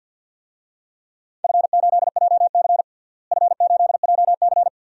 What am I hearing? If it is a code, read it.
Morse code, heard as a keyed tone, about 50 wpm, 700 Hz. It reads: F8JC F8JC
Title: DXCC-100 Call Signs - 4 Characters - Repeated 2x - 50wpm